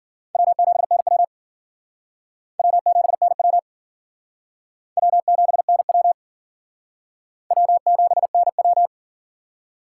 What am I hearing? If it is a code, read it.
Morse code, heard as a keyed tone, about 51 wpm, 700 Hz. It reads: W7NW W7NW W7NW W7NW